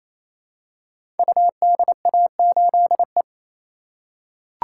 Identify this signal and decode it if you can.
Morse code, heard as a keyed tone, about 28 wpm, 700 Hz. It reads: UDA8E
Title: Code Groups - 5 Letter:Numbers - 28wpm